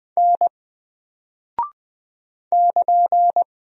Morse code, heard as a keyed tone, about 20 wpm, 700 Hz. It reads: N <KN>
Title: Single Character (Letter:Number:Punctuation:Prosign) 20wpm